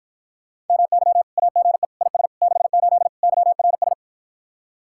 Morse code, heard as a keyed tone, about 54 wpm, 700 Hz. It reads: MY AGE IS 68 YRS